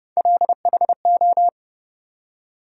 Morse code, heard as a keyed tone, about 30 wpm, 700 Hz. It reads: LHO
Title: USA Call Sign Postfixes - 3 Letters 30wpm